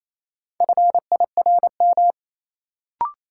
Morse code, heard as a keyed tone, about 28 wpm, 700 Hz. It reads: FIRM
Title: Sentences from Top 1400 Words - New Word Drill - 28wpm